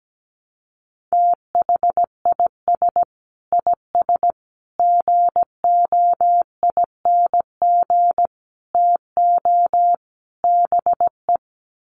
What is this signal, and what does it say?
Morse code, heard as a keyed tone, about 17 wpm, 700 Hz. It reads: THIS IS GOING TO BE
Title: Sets of 5 Words 17wpm